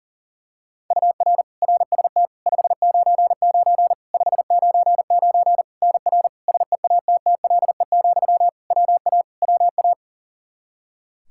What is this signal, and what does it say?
Morse code, heard as a keyed tone, about 40 wpm, 700 Hz. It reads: UR RST 599 599 NR SEATTLE, WA WA
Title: Practice QSO - Exchange 1 - Beginning - 40wpm